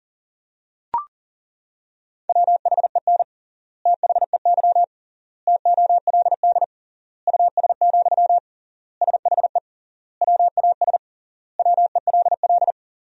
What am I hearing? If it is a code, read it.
Morse code, heard as a keyed tone, about 40 wpm, 700 Hz. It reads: WHEN THEY TOLD US, SHE WAS WELL
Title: 3x Word-Spacing - Sentences from Top 400 Words - 40wpm